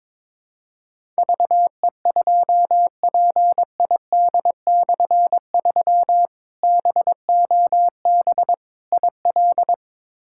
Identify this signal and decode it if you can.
Morse code, heard as a keyed tone, about 22 wpm, 700 Hz. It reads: VE2PID/3 BOB IL
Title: K1USN SST QSO Practice - 22wpm